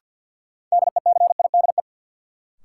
Morse code, heard as a keyed tone, about 51 wpm, 700 Hz. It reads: DECIDE
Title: Sentences from Top 500 Words - New Word Drill - 50wpm